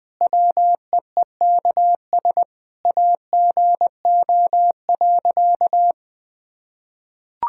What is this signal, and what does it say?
Morse code, heard as a keyed tone, about 21 wpm, 700 Hz. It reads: WEEKS AGO.
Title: Sentences with 5-Word Phrases - 20wpm